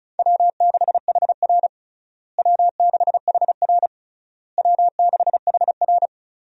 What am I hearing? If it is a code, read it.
Morse code, heard as a keyed tone, about 36 wpm, 700 Hz. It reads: W6HR W6HR W6HR
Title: USA Call Signs - 4 Characters - Repeated 3x - 35wpm